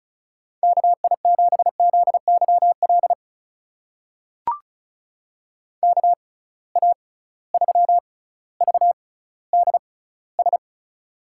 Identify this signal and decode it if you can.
Morse code, heard as a keyed tone, about 36 wpm, spaced as for 18 wpm, 700 Hz. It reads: KI7ZYL KA3VDS
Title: USA Call Signs - 6 Characters - 6x Character Spacing - 35wpm